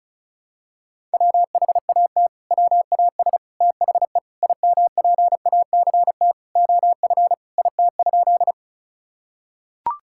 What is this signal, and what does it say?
Morse code, heard as a keyed tone, about 36 wpm, 700 Hz. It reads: WHAT WAS THE IMPACT OF IT?
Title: Sentences from Top 1000 Words - Review Part 6 - 35wpm